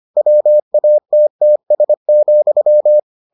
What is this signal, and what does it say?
Morse code, heard as a keyed tone, about 25 wpm, 600 Hz. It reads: WATTS,